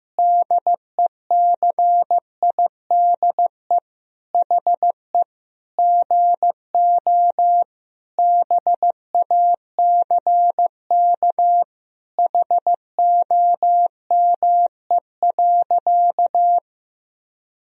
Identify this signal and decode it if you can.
Morse code, heard as a keyed tone, about 15 wpm, 700 Hz. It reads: DECIDE HE GO BACK HOME.
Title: Sentences from Top 500 Words and Abbreviations 15wpm